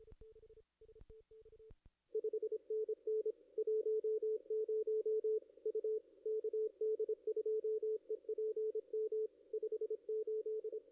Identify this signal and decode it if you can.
Morse code, heard as a keyed tone, about 25 wpm, 440 Hz. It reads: 5NN 10 U KD2EPM 58
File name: contests / ss-cw-k5tr-op-n5rz -2300